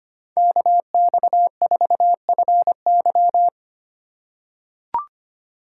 Morse code, heard as a keyed tone, about 25 wpm, 700 Hz. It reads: KX4FY
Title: USA Call Signs - 5 Characters - 3x Character Spacing - 25wpm